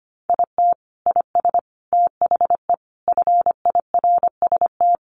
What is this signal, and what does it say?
Morse code, heard as a keyed tone, about 25 wpm, 700 Hz. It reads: IT IS THE FIRST
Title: Additional Easy Phrases 25wpm